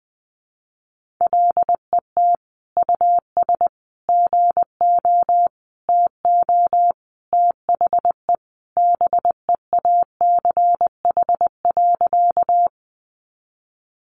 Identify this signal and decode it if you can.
Morse code, heard as a keyed tone, about 20 wpm, 700 Hz. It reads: LET US GO TO THE BEACH.
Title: Sentences from Top 1500 Words 20wpm